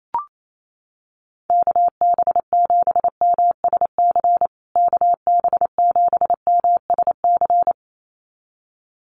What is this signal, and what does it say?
Morse code, heard as a keyed tone, about 28 wpm, 700 Hz. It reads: KB7MSC KB7MSC
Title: USA Call Signs Repeated 2x 28wpm